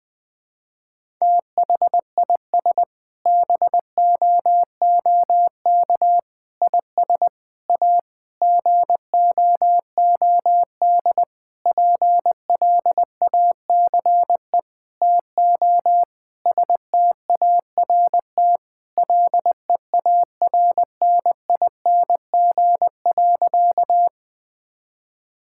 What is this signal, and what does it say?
Morse code, heard as a keyed tone, about 20 wpm, 700 Hz. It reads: THIS BOOK IS A GOOD PLACE TO START LEARNING.
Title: Sentences with 6-Word Phrases - Spoken Beginning - 20wpm